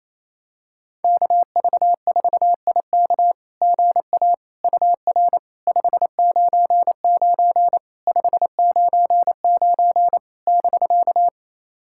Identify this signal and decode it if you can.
Morse code, heard as a keyed tone, about 28 wpm, 690 Hz. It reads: KV4IK GA UR 599 599 <BK>
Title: SOTA QSO Practice 28wpm